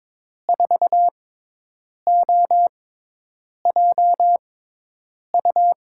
Morse code, heard as a keyed tone, about 22 wpm, spaced as for 8 wpm, 700 Hz. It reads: 4OJU
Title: USA Call Signs - 6 Characters - 6x Character Spacing - 22wpm